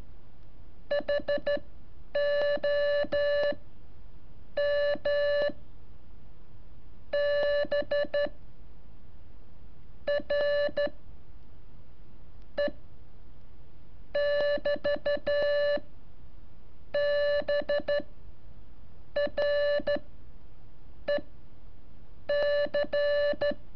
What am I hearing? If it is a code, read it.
Morse code, heard as a keyed tone, about 13 wpm, spaced as for 5 wpm, 580 Hz. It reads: HOMBRE <BT> BREC
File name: curso / ejercicio 17